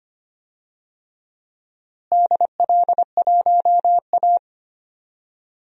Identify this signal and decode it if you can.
Morse code, heard as a keyed tone, about 25 wpm, 700 Hz. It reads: DL1A
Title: DXCC-100 Call Signs - 4 Characters - Repeated 3x - 25wpm